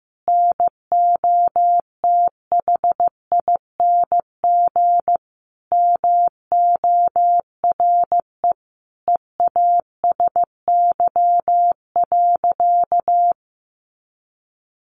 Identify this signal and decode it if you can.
Morse code, heard as a keyed tone, about 15 wpm, 700 Hz. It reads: NOTHING MORE EASY.
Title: Sentences from Top 700 Words 15wpm